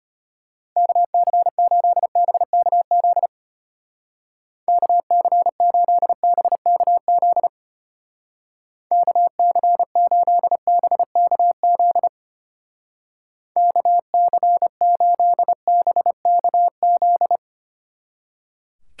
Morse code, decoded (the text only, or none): KC8BKZ KC8BKZ KC8BKZ KC8BKZ